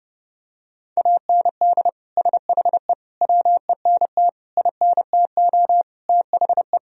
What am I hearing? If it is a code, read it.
Morse code, heard as a keyed tone, about 30 wpm, 700 Hz. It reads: AND SHE WENT INTO THE
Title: Sentences from Top 1000 Words - Review Part 2 - 30wpm